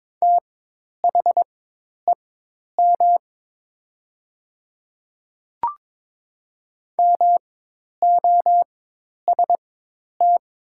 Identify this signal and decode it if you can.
Morse code, heard as a keyed tone, about 22 wpm, spaced as for 10 wpm, 700 Hz. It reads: THEM MOST
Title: 4x Character Spacing - Top 100 Words - 22wpm